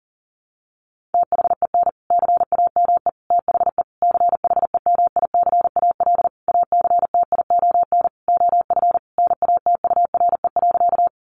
Morse code, heard as a keyed tone, about 40 wpm, 700 Hz. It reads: THEN CAME THE CHEMICAL ACTION OF NATURE.
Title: Sentences from Top 1200 Words 40wpm